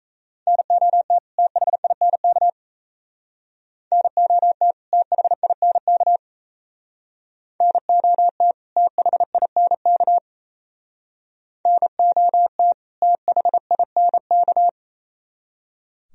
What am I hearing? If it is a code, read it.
Morse code, heard as a keyed tone, about 36 wpm, 700 Hz. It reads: NOT THINK NOT THINK NOT THINK NOT THINK